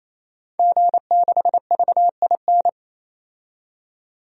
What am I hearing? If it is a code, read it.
Morse code, heard as a keyed tone, about 28 wpm, 700 Hz. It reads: G6VIN
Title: DXCC-100 Call Signs - 5 Characters - 28wpm